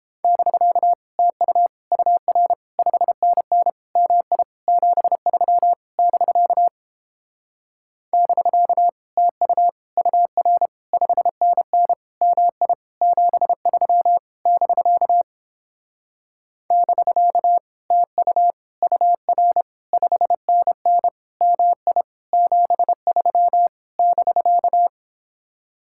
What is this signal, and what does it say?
Morse code, heard as a keyed tone, about 29 wpm, 700 Hz. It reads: <BK> TU UR 5NN MI 73 <BK> <BK> TU UR 5NN MI 73 <BK> <BK> TU UR 5NN MI 73 <BK>